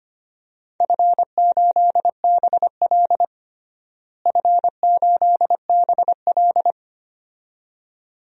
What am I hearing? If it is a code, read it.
Morse code, heard as a keyed tone, about 25 wpm, 700 Hz. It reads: F8BL F8BL
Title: DXCC-100 Call Signs - 4 Characters - Repeated 2x - 25wpm